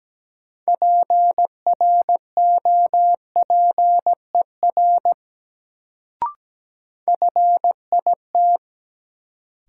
Morse code, heard as a keyed tone, about 17 wpm, 700 Hz. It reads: PROPER FIT
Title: Sentences from Top 800 Words - New Word Drill - 17wpm